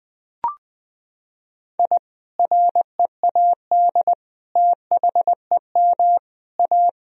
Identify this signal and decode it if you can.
Morse code, heard as a keyed tone, about 20 wpm, 700 Hz. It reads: I READ THEM A
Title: Sentences from Top 1000 Words - Review Part 1 - 20wpm